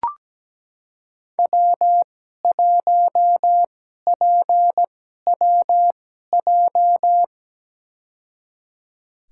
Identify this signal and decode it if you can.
Morse code, heard as a keyed tone, about 17 wpm, spaced as for 12 wpm, 700 Hz. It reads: W1PWJ
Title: USA Call Signs - 5 Characters - 2x Character Spacing - 17wpm